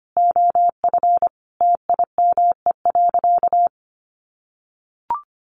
Morse code, heard as a keyed tone, about 25 wpm, 700 Hz. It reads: OF TIME.
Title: Sentences from Top 1300 Words 25wpm